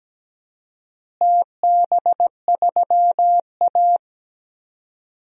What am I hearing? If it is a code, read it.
Morse code, heard as a keyed tone, about 17 wpm, 700 Hz. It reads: TB3A
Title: DXCC-100 Call Signs - 4 Characters - Repeated 2x - 17wpm